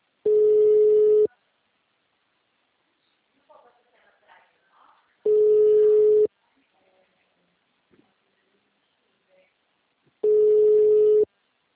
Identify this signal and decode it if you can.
Ringback tone (Europe, 425 Hz), with speech.